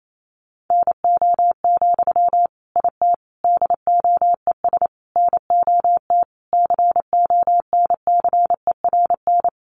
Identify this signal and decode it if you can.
Morse code, heard as a keyed tone, about 28 wpm, 700 Hz. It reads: NO, IT DOES NOT CONCERN
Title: Sentences from Top 1000 Words 28wpm